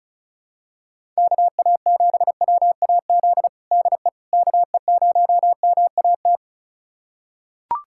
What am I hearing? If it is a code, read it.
Morse code, heard as a keyed tone, about 36 wpm, 700 Hz. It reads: KA7WAZ DE KE0MAT
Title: USA Callsign DE Callsign - 6 Characters - 35wpm